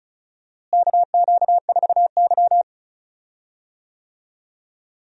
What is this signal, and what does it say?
Morse code, heard as a keyed tone, about 36 wpm, 700 Hz. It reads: KQ4Y
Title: USA Call Signs - 4 Characters - Repeated 2x - 35wpm